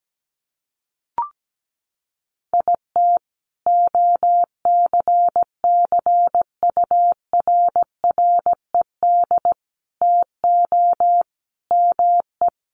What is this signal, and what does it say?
Morse code, heard as a keyed tone, about 17 wpm, 700 Hz. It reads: IT OCCURRED TO ME
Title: Sets of 5 Words Encore 17wpm